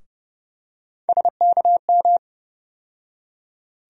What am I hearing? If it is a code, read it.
Morse code, heard as a keyed tone, about 30 wpm, 700 Hz. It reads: SKM
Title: USA Call Sign Postfixes - 3 Letters 30wpm